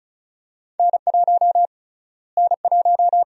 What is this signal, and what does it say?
Morse code, heard as a keyed tone, about 36 wpm, 700 Hz. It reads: N1 N1
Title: USA Call Sign Prefixes - Repeated 2x - 35wpm